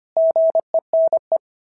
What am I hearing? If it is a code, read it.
Morse code, heard as a keyed tone, about 25 wpm, 690 Hz. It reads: GENE
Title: SOTA QSO Practice 25wpm